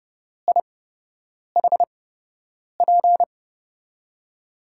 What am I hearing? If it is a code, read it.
Morse code, heard as a keyed tone, about 30 wpm, spaced as for 8 wpm, 700 Hz. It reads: IHP
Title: USA Call Signs - 5 Characters - 8x Character Spacing - 30wpm